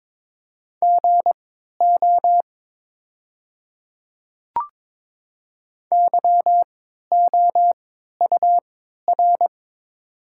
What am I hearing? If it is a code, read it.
Morse code, heard as a keyed tone, about 22 wpm, spaced as for 12 wpm, 700 Hz. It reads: GO YOUR
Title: 3x Character Spacing - Top 100 Words - 22wpm